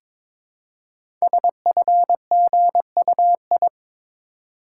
Morse code, heard as a keyed tone, about 22 wpm, 700 Hz. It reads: SFGUI
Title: Code Groups - 5 Letter:Numbers - 22wpm